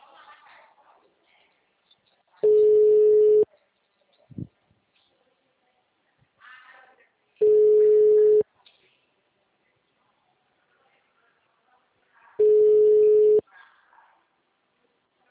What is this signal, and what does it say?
Ringback tone (Europe, 425 Hz).
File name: out- -2004- - - .209